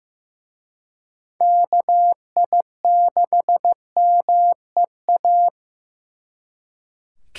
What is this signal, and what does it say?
Morse code, heard as a keyed tone, about 18 wpm, 700 Hz. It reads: KI6MEA
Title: Speed Racing - USA 2x3 Call Signs 15wpm